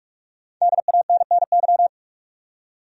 Morse code, heard as a keyed tone, about 41 wpm, 700 Hz. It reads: DANNY
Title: USA Common Names - 45wpm